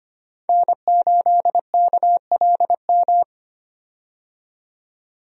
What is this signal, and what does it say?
Morse code, heard as a keyed tone, about 25 wpm, 700 Hz. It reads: N8KLM
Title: USA Call Signs - 5 Characters - 25wpm